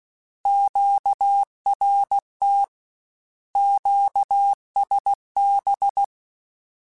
Morse code, heard as a keyed tone, about 16 wpm, 800 Hz. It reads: QRT QSB
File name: Short Runs / Short Runs-mp3s / Words @15